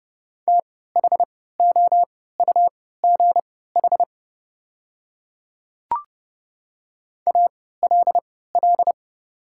Morse code, heard as a keyed tone, about 30 wpm, spaced as for 17 wpm, 690 Hz. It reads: THOUGH ALL
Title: 3x Character Spacing - Top 500 Words - 30wpm